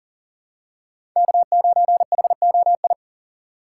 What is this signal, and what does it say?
Morse code, heard as a keyed tone, about 38 wpm, 700 Hz. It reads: K9HOI
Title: USA Call Signs - 5 Characters - 2x Character Spacing - 40wpm